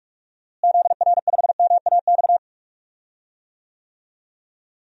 Morse code, heard as a keyed tone, about 46 wpm, 700 Hz. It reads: ZR5MAX
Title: DXCC-100 Call Signs - 6 Characters - Repeated 3x - 45wpm